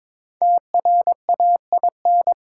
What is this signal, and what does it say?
Morse code, heard as a keyed tone, about 22 wpm, 700 Hz. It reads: TRAIN
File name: 5 Letter Words - First 3 Letters Given - 22wpm